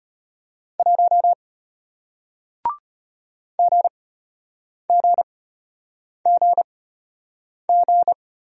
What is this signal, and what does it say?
Morse code, heard as a keyed tone, about 33 wpm, spaced as for 8 wpm, 700 Hz. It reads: 1 GGGG